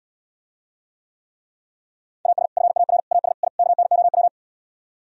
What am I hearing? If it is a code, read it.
Morse code, heard as a keyed tone, about 77 wpm, 690 Hz. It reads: AS FAR AS I KNOW